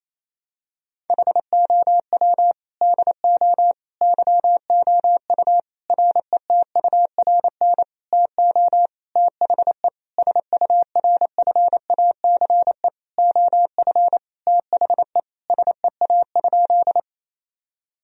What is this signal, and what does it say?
Morse code, heard as a keyed tone, about 28 wpm, 700 Hz. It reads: HOW DO YOU RETURN TO THE SURFACE OF THE SEA?
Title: Sentences from Top 1000 Words - Review Part 3 - 28wpm